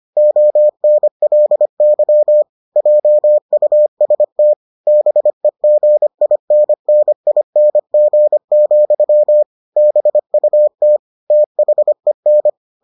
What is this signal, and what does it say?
Morse code, heard as a keyed tone, about 25 wpm, 600 Hz. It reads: ONLY JUST BEGINNING, BUT THEN